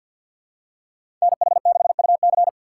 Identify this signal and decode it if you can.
Morse code, heard as a keyed tone, about 53 wpm, 700 Hz. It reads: NH6UC
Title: USA Call Signs - 5 Characters - 6x Character Spacing - 50wpm